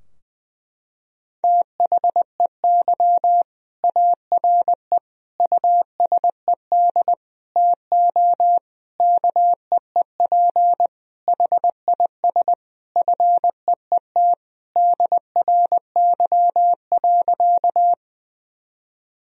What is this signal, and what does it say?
Morse code, heard as a keyed tone, about 20 wpm, 700 Hz. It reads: THEY ARE USED TO KEEP HIS FEET DRY.
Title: Sentences from Top 1000 Words - Review Part 3 - 20wpm